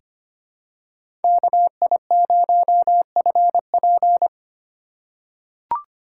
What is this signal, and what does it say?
Morse code, heard as a keyed tone, about 25 wpm, 700 Hz. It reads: KI0FP
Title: USA Call Signs - 5 Characters - 4x Character Spacing - 25wpm